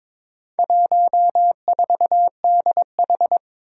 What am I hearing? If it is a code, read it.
Morse code, heard as a keyed tone, about 22 wpm, 700 Hz. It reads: 14DH
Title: Four Challenging Letter-Number Combinations 22wpm